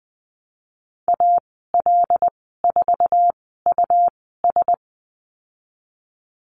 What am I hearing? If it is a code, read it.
Morse code, heard as a keyed tone, about 20 wpm, spaced as for 15 wpm, 700 Hz. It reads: AL4US
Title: 140 Lesson 23g - Letter f - Call Signs 20wpm